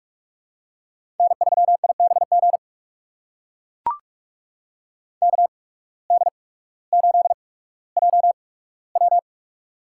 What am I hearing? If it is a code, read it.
Morse code, heard as a keyed tone, about 44 wpm, spaced as for 19 wpm, 700 Hz. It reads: N3IBG KD8JW
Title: USA Call Signs - 8x Character Spacing - 45wpm